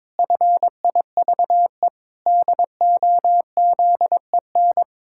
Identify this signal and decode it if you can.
Morse code, heard as a keyed tone, about 22 wpm, 700 Hz. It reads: FIVE DOZEN
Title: Pangram Challenge v2 - 22wpm